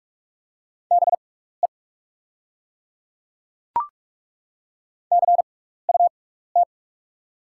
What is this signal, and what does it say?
Morse code, heard as a keyed tone, about 42 wpm, spaced as for 15 wpm, 700 Hz. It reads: BE CUT
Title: 6x Character Spacing - Top 500 Words - 45wpm